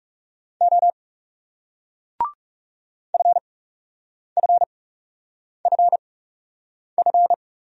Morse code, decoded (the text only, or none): O FFFF